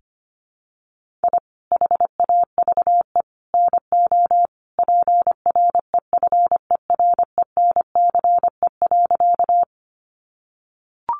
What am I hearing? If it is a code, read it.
Morse code, heard as a keyed tone, about 25 wpm, 700 Hz. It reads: I HAVE NO PREFERENCE.
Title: Sentences from Top 1500 Words 25wpm